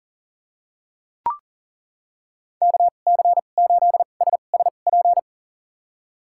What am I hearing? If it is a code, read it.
Morse code, heard as a keyed tone, about 40 wpm, spaced as for 29 wpm, 700 Hz. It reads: KC8SSP
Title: USA Call Signs - 6 Characters - 2x Character Spacing - 40wpm